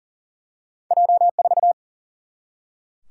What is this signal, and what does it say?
Morse code, heard as a keyed tone, about 40 wpm, 700 Hz. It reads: J4
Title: Single-Letter Single-Number 40wpm